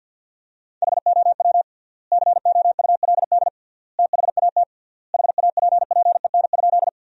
Morse code, heard as a keyed tone, about 50 wpm, 700 Hz. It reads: HOW COULD THAT HAPPEN?
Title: 3x Word-Spacing - Sentences from Top 400 Words - 50wpm